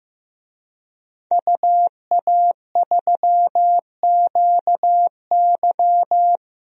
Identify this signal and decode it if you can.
Morse code, heard as a keyed tone, about 15 wpm, 700 Hz. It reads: UA3QY
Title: DXCC-100 Call Signs - 5 Characters - 15wpm